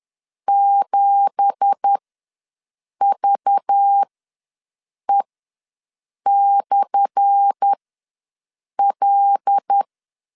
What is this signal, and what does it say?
Morse code, heard as a keyed tone, about 10 wpm, spaced as for 6 wpm, 800 Hz. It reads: 7VE/L
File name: LEZIONI / 10 SIGNS / 10D COMA ws 10wpm 4314